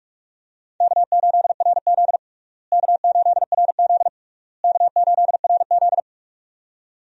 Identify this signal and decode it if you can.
Morse code, heard as a keyed tone, about 41 wpm, 700 Hz. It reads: K8RZ K8RZ K8RZ
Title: USA Call Signs - 4 Characters - Repeated 3x - 45wpm